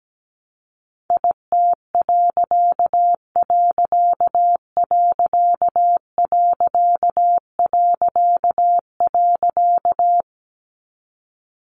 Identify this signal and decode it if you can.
Morse code, heard as a keyed tone, about 17 wpm, 700 Hz. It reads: IT......
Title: A Sound Of Thunder 17wpm